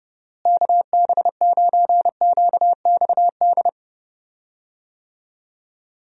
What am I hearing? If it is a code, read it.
Morse code, heard as a keyed tone, about 29 wpm, 700 Hz. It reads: KB9QXD
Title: USA Call Signs - 6 Characters - Repeated 3x - 30wpm